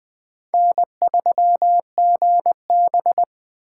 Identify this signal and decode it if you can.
Morse code, heard as a keyed tone, about 21 wpm, 700 Hz. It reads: N3GB
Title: USA Call Signs - 4 Characters - 3x Character Spacing - 20wpm